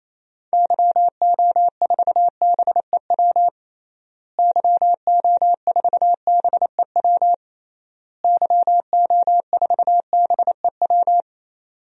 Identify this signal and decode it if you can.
Morse code, heard as a keyed tone, about 28 wpm, 700 Hz. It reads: YO4BEW YO4BEW YO4BEW
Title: DXCC-100 Call Signs - 6 Characters - Repeated 3x - 28wpm